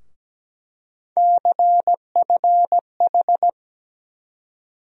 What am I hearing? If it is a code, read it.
Morse code, heard as a keyed tone, about 17 wpm, 700 Hz. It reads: CFH
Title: USA Call Sign Postfixes - 3 Letters 17wpm